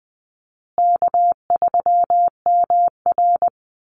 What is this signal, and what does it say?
Morse code, heard as a keyed tone, about 20 wpm, 700 Hz. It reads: K3MR
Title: Call Signs Repeated 3x - 20wpm